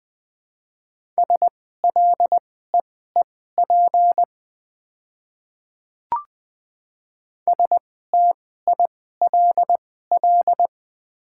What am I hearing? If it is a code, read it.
Morse code, heard as a keyed tone, about 20 wpm, spaced as for 15 wpm, 700 Hz. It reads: SLEEP STILL
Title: 2x Character Spacing - Top 500 Words - 20wpm